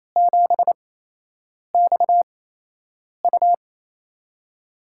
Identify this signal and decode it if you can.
Morse code, heard as a keyed tone, about 28 wpm, 700 Hz. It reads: 7 X U
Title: USA Call Signs - 4 Characters - 8x Character Spacing - 28wpm